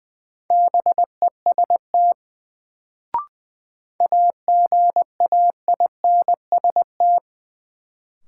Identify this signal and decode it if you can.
Morse code, heard as a keyed tone, about 20 wpm, 700 Hz. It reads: BEST AGAINST
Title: Sets of 2 Words - New Word Drill - 20wpm